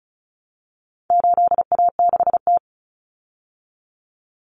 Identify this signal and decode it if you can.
Morse code, heard as a keyed tone, about 34 wpm, 700 Hz. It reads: 8A6T
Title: Common Cut Numbers - 4-Digit 35wpm